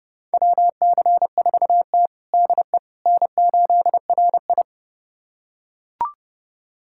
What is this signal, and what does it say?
Morse code, heard as a keyed tone, about 30 wpm, 700 Hz. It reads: WC4T DE N8RI
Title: USA Callsign DE Callsign - 4 Characters - 30wpm